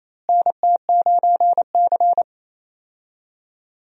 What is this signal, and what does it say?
Morse code, heard as a keyed tone, about 28 wpm, 700 Hz. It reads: NT9C
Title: USA Call Signs - 4 Characters - Repeated 2x - 28wpm